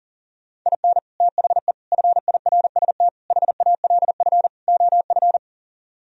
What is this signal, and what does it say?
Morse code, heard as a keyed tone, about 51 wpm, 700 Hz. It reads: IN THE FIRST HALF OF